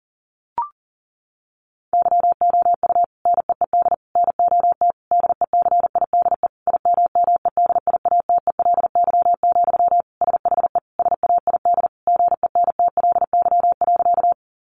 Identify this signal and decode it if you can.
Morse code, heard as a keyed tone, about 37 wpm, 700 Hz. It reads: YOU NEED NOT DECIDE IMMEDIATELY, SHE SAID GENTLY.
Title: Sentences from Top 1500 Words 40wpm